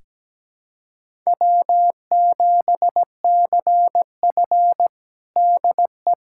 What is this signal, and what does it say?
Morse code, heard as a keyed tone, about 17 wpm, 700 Hz. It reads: W7CF DE
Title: USA Callsign DE Callsign - 4 Characters - 17wpm